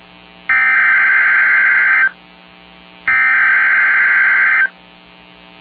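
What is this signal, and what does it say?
Emergency Alert System (SAME) tones (header bursts), heard as harsh buzzing bursts. It is heard from 0.5 to 4.6 s.
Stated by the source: SAME tones from Weather Radio KGG95 (Winona, MN), National Weather Service office in La Crosse.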